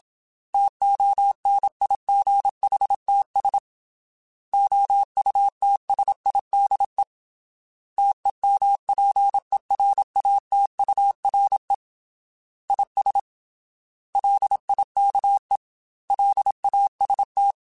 Morse code, heard as a keyed tone, about 27 wpm, 800 Hz. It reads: TONIGHTS OUTSIDE TEMPERATURE IS LIKE LAST
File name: Weather Sets / Weather Sets-mp3s / Weather 4 17 @25